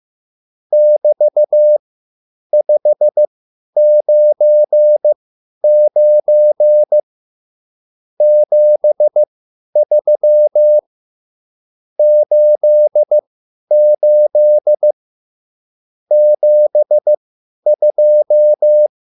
Morse code, heard as a keyed tone, about 15 wpm, spaced as for 10 wpm, 600 Hz. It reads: <BT>599 73 88 72